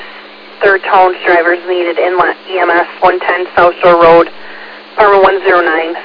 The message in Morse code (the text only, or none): none